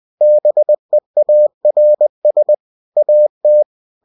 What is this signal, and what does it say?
Morse code, heard as a keyed tone, about 20 wpm, 600 Hz. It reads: BEARS AT